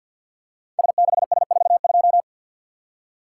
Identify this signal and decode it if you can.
Morse code, heard as a keyed tone, about 52 wpm, 690 Hz. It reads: S6S42
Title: Code Groups - 5 Letter:Numbers - 50wpm